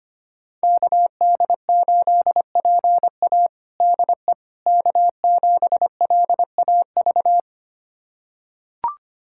Morse code, heard as a keyed tone, about 25 wpm, 700 Hz. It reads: KD8PA DE K7LAV
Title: USA Callsign DE Callsign - 5 Characters - 25wpm